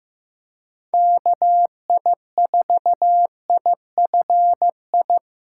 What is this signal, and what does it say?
Morse code, heard as a keyed tone, about 18 wpm, 700 Hz. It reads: KI4IFI
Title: Speed Racing - USA 2x3 Call Signs 15wpm